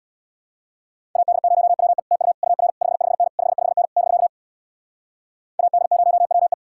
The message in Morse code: AD0WE TU UR 55N 55N <BK> AD0WE

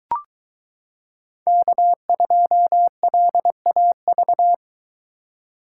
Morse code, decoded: K2LAV

